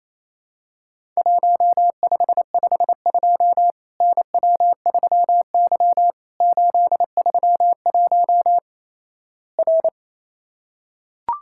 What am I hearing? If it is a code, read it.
Morse code, heard as a keyed tone, about 28 wpm, 700 Hz. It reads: 1552 NW3Y 831 R
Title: Worked All Europe DX Contest - QTC Exchange - 28wpm